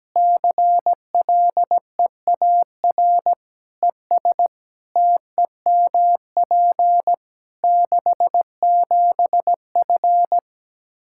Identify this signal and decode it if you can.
Morse code, heard as a keyed tone, about 17 wpm, 700 Hz. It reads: CLEAR ES TEMP 67F
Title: K1USN SST QSO Practice 17wpm